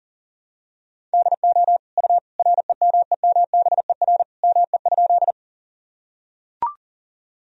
Morse code, heard as a keyed tone, about 40 wpm, 700 Hz. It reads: DO U REMEMBER ME?
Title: Sentences from Top 400 Words and Abbreviations 40wpm